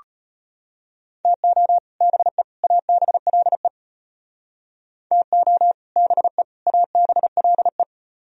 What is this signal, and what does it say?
Morse code, heard as a keyed tone, about 33 wpm, 700 Hz. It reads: TO BE ABLE TO BE ABLE